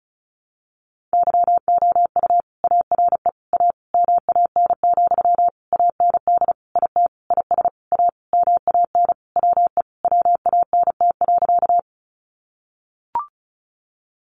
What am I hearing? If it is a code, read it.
Morse code, heard as a keyed tone, about 34 wpm, 700 Hz. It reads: YOU ARE A MAN, AND IT IS A MAN WE WANT.
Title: Sentences from Top 200 Words 35wpm